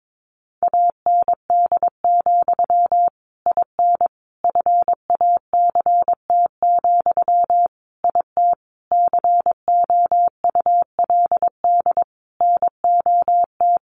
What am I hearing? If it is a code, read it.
Morse code, heard as a keyed tone, about 22 wpm, 700 Hz. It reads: AND, IN FACT, IT COULD NOT
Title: Sentences from Top 1200 Words 22wpm